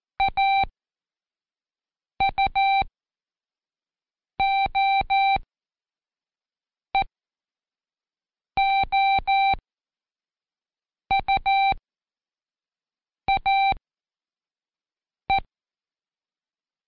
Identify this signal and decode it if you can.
Morse code, heard as a keyed tone, about 14 wpm, spaced as for 5 wpm, 780 Hz. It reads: AUOEOUAE
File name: Seg29-Day4-CheckTest 4